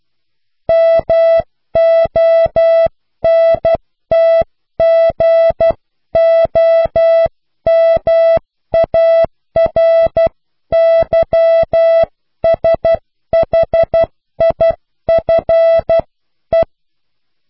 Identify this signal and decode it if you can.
Morse code, heard as a keyed tone, about 12 wpm, 660 Hz. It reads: MONTGOMARYSHIFE
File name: cw / Pseudo woorden (15)